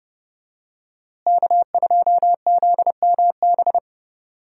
Morse code, heard as a keyed tone, about 30 wpm, 700 Hz. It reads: K2ZMB
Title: USA Call Signs - 5 Characters - Repeated 3x - 30wpm